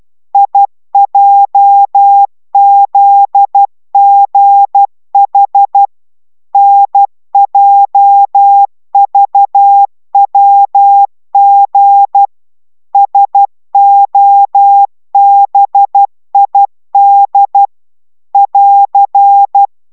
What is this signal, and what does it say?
Morse code, heard as a keyed tone, about 12 wpm, 800 Hz. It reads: IJZGH NJVWG SOBID <AR>